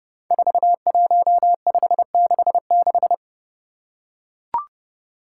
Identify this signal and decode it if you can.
Morse code, heard as a keyed tone, about 30 wpm, 700 Hz. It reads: 41566
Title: Five Challenging Letter-Number Combinations 30wpm